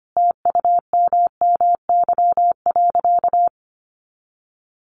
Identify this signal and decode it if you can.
Morse code, heard as a keyed tone, about 25 wpm, 700 Hz. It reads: TUMMY.
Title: Easy Phrases 25wpm